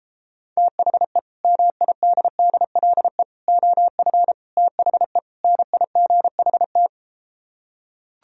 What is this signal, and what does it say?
Morse code, heard as a keyed tone, about 33 wpm, 700 Hz. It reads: THE MIDDLE OF THE NIGHT